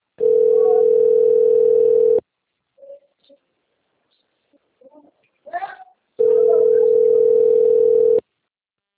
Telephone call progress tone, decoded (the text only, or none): ringback tone